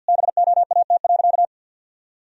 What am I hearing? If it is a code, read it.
Morse code, heard as a keyed tone, about 51 wpm, 700 Hz. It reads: BOAT.
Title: Sentences from Top 1000 Words - Review Part 3 - 50wpm